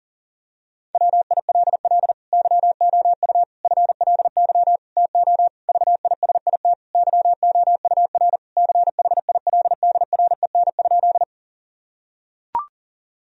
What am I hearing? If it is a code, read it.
Morse code, heard as a keyed tone, about 40 wpm, 700 Hz. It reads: WILL YOU FLY TO VISIT YOUR CHILDREN?
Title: Sentences from Top 1000 Words - Review Part 2 - 40wpm